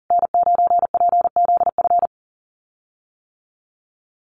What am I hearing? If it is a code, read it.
Morse code, heard as a keyed tone, about 37 wpm, 700 Hz. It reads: N9PZF
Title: USA Call Signs Repeated 3x 40wpm